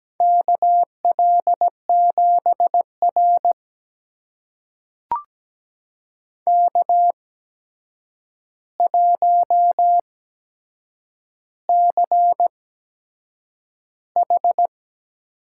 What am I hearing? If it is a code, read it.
Morse code, heard as a keyed tone, about 17 wpm, spaced as for 7 wpm, 700 Hz. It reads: KL7R K1CH
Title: USA Call Signs - 4 Characters - 8x Character Spacing - 17wpm